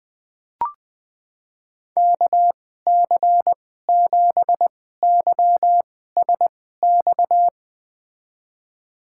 Morse code, heard as a keyed tone, about 20 wpm, spaced as for 14 wpm, 700 Hz. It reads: KC7YSX
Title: USA Call Signs - 6 Characters - 2x Character Spacing - 20wpm